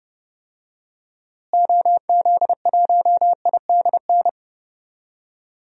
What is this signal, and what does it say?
Morse code, heard as a keyed tone, about 30 wpm, 700 Hz. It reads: OZ1IDN
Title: DXCC-100 Call Signs - 6 Characters - Repeated 3x - 30wpm